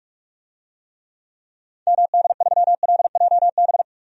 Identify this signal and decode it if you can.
Morse code, heard as a keyed tone, about 46 wpm, 700 Hz. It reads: MD3LJB
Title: DXCC-100 Call Signs - 6 Characters - 45wpm